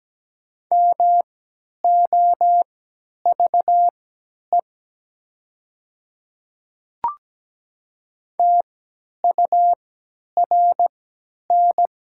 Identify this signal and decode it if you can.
Morse code, heard as a keyed tone, about 17 wpm, spaced as for 10 wpm, 700 Hz. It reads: MOVE TURN
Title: 3x Character Spacing - Top 200 Words - 17wpm